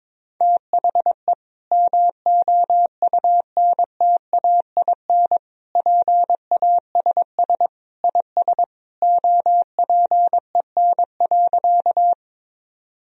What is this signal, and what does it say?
Morse code, heard as a keyed tone, about 22 wpm, 700 Hz. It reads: THE MOUNTAIN PASS IS OPEN.
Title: Sentences from Top 1000 Words - Review Part 1 - 22wpm